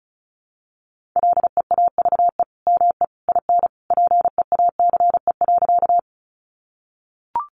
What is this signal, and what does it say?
Morse code, heard as a keyed tone, about 34 wpm, 700 Hz. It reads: LEAVE ME IN PEACE.
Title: Sentences from Top 1500 Words 35wpm